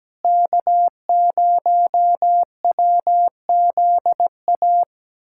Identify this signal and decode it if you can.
Morse code, heard as a keyed tone, about 17 wpm, 700 Hz. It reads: K0WZA
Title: USA Callsign DE Callsign 17wpm